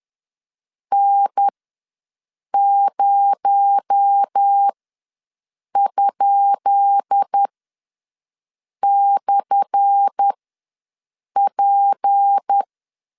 Morse code, heard as a keyed tone, about 10 wpm, spaced as for 6 wpm, 800 Hz. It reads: N0?/P
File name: LEZIONI / 10 SIGNS / 10C DOT ws 10wpm 4213